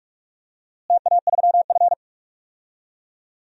Morse code, heard as a keyed tone, about 41 wpm, 700 Hz. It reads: TA3F
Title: DXCC-100 Call Signs - 4 Characters - Repeated 2x - 45wpm